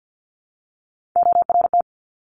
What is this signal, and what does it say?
Morse code, heard as a keyed tone, about 50 wpm, 690 Hz. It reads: ORT
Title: Top Three Letter Combinations in Words 50wpm